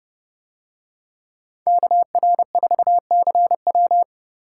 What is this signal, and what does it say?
Morse code, heard as a keyed tone, about 28 wpm, 700 Hz. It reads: KR4CW